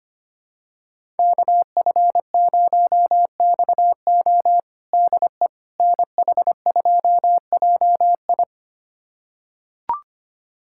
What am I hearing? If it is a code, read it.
Morse code, heard as a keyed tone, about 25 wpm, 700 Hz. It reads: KF0XO DE NH2JI
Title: USA Callsign DE Callsign - 5 Characters - 25wpm